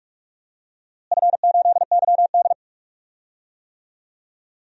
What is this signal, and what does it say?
Morse code, heard as a keyed tone, about 44 wpm, 700 Hz. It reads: F8YD
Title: DXCC-100 Call Signs - 4 Characters - Repeated 2x - 45wpm